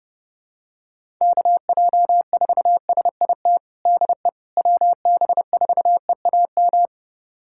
Morse code, heard as a keyed tone, about 30 wpm, 700 Hz. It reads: KJ4SIT DE WB4EAM
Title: USA Callsign DE Callsign - 6 Characters - 30wpm